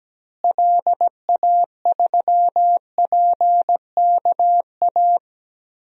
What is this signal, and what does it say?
Morse code, heard as a keyed tone, about 17 wpm, 700 Hz. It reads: LA3PKA
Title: DXCC-100 Call Signs - 6 Characters - Repeated 2x - 17wpm